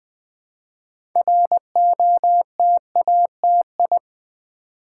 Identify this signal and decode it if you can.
Morse code, heard as a keyed tone, about 20 wpm, 700 Hz. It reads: ROTATI
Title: Build A Word - 3 Letters At A Time - 8-Letter Words - 20wpm